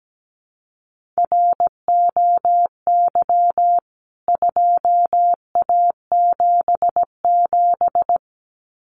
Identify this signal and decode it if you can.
Morse code, heard as a keyed tone, about 17 wpm, 700 Hz. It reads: ROY 2A77
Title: CWT Exchange 17wpm